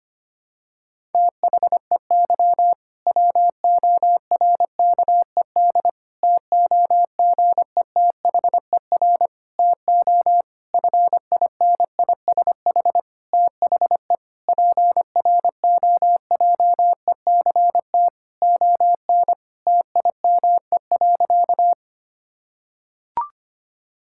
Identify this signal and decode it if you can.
Morse code, heard as a keyed tone, about 25 wpm, 700 Hz. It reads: THEY WORKED TOGETHER TO FINISH THE PROJECT ON TIME.
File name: practice / 25wpm / Difficult Sentences with Hints - 25wpm